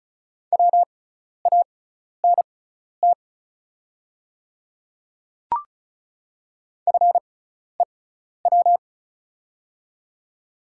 Morse code, heard as a keyed tone, about 34 wpm, spaced as for 12 wpm, 690 Hz. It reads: WANT FEW
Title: 6x Character Spacing - Top 300 Words - 35wpm